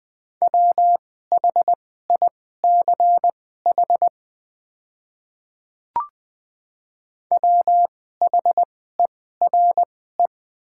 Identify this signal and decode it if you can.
Morse code, heard as a keyed tone, about 20 wpm, spaced as for 15 wpm, 700 Hz. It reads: WHICH WHERE
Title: 2x Character Spacing - Top 500 Words - 20wpm